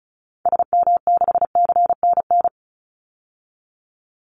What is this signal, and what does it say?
Morse code, heard as a keyed tone, about 34 wpm, 700 Hz. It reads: SM6CNN
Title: Call Signs - 35wpm